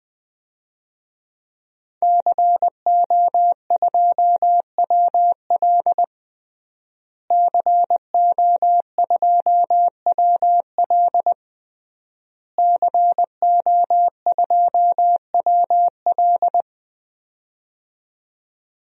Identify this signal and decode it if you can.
Morse code, heard as a keyed tone, about 21 wpm, 700 Hz. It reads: CO2WL CO2WL CO2WL
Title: DXCC-100 Call Signs - 5 Characters - Repeated 3x - 20wpm